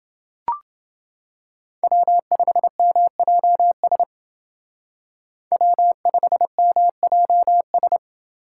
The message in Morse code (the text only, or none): W5MJS W5MJS